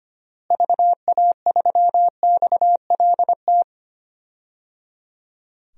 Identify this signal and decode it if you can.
Morse code, heard as a keyed tone, about 25 wpm, 700 Hz. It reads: VA3XLT
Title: DXCC-100 Call Signs - 6 Characters - Repeated 3x - 25wpm